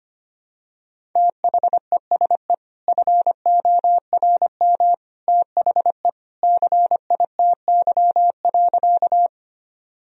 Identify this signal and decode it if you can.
Morse code, heard as a keyed tone, about 25 wpm, 700 Hz. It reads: THESE FORM THE CITY.
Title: Sentences from Top 1000 Words - Review Part 1 - 25wpm